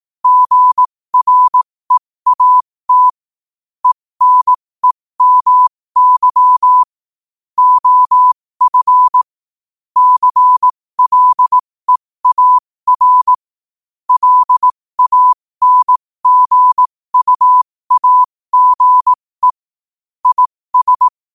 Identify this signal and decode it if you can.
Morse code, heard as a keyed tone, about 18 wpm, spaced as for 15 wpm, 1000 Hz. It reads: GREAT ENEMY OF CLEAR LANGUAGE IS
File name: qotdfiles / 15wpm